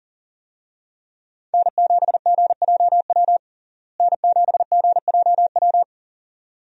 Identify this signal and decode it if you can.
Morse code, heard as a keyed tone, about 40 wpm, 700 Hz. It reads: N7GJW N7GJW
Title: USA Call Signs - 5 Characters - Repeated 2x - 40wpm